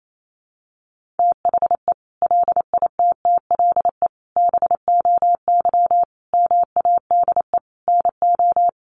Morse code, heard as a keyed tone, about 28 wpm, 700 Hz. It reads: THE LITTLE BOY MADE NO
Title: Sentences from Top 1400 Words 28wpm